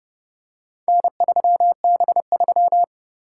Morse code, heard as a keyed tone, about 30 wpm, 700 Hz. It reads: N3B3